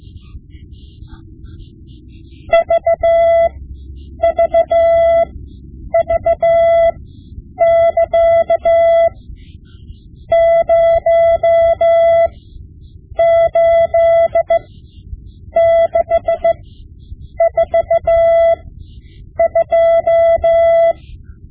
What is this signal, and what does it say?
Morse code, heard as a keyed tone, about 15 wpm, spaced as for 8 wpm, 670 Hz. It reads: VVV<KA> 08642
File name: Morse Lessons / Morse Lessons / Tape-5B